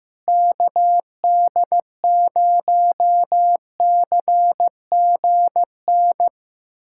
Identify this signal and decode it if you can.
Morse code, heard as a keyed tone, about 15 wpm, 700 Hz. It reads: KD0CGN
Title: USA Callsign DE Callsign - 6 Characters - 15wpm